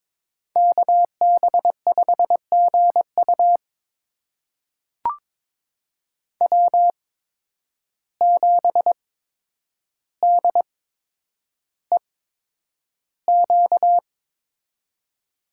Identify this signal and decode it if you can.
Morse code, heard as a keyed tone, about 22 wpm, spaced as for 9 wpm, 700 Hz. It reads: KB5GU W7DEQ
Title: USA Call Signs - 5 Characters - 8x Character Spacing - 22wpm